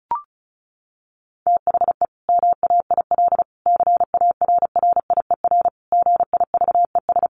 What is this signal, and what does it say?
Morse code, heard as a keyed tone, about 34 wpm, 700 Hz. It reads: THE MAIL CARRIER GIVES
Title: Additional Easy Phrases 35wpm